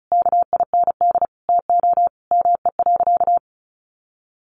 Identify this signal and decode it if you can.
Morse code, heard as a keyed tone, about 34 wpm, 700 Hz. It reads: KIND TO ME.
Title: Sentences from Top 1100 Words 35wpm